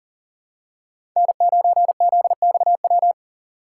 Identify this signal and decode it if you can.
Morse code, heard as a keyed tone, about 40 wpm, 700 Hz. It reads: N9ZXW
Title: USA Call Signs - 5 Characters - 4x Character Spacing - 40wpm